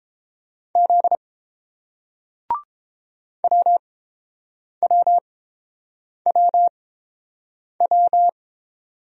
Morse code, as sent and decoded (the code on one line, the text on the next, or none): --.. / .-- .-- .-- .--
Z WWWW